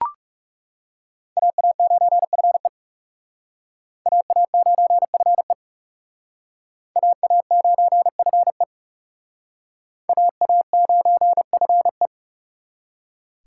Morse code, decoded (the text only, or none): AA9FE AA9FE AA9FE AA9FE